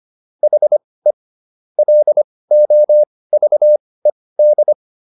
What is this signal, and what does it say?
Morse code, heard as a keyed tone, about 25 wpm, spaced as for 18 wpm, 600 Hz. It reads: HE LOVED